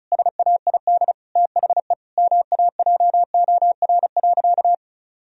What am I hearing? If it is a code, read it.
Morse code, heard as a keyed tone, about 36 wpm, 700 Hz. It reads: SAID THE MAJOR.
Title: Sentences from Top 1000 Words - Review Part 4 - 35wpm